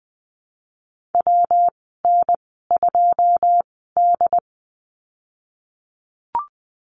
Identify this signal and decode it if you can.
Morse code, heard as a keyed tone, about 20 wpm, spaced as for 15 wpm, 700 Hz. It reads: WN2D
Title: 122 Lesson 20g - Number 3 - Call Signs 20wpm